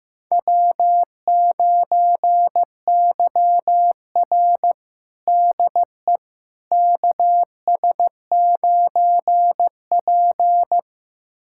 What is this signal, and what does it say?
Morse code, heard as a keyed tone, about 15 wpm, 700 Hz. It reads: W9YR DE KS9P
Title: USA Callsign DE Callsign - 4 Characters - 15wpm